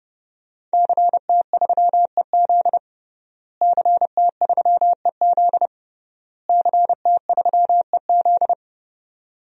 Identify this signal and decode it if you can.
Morse code, heard as a keyed tone, about 32 wpm, 700 Hz. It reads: CT3EZ CT3EZ CT3EZ
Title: DXCC-100 Call Signs - 5 Characters - Repeated 3x - 30wpm